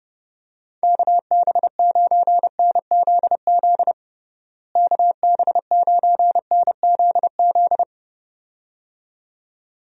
Morse code, heard as a keyed tone, about 30 wpm, 700 Hz. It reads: KB9NZZ KB9NZZ
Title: USA Call Signs - 6 Characters - Repeated 2x - 30wpm